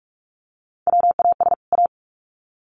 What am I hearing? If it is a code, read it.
Morse code, heard as a keyed tone, about 44 wpm, 700 Hz. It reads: WAS A
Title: Sets of 2 Words 45wpm